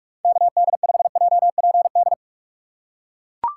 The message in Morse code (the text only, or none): KD5JPD